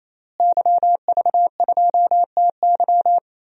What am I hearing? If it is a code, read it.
Morse code, heard as a keyed tone, about 28 wpm, 700 Hz. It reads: YV2TY
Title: DXCC-100 Call Signs - 5 Characters - Repeated 3x - 28wpm